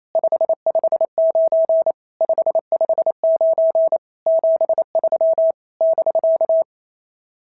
Morse code, heard as a keyed tone, about 27 wpm, 650 Hz. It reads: 559 559 73 <BK>
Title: SOTA QSO Practice 28wpm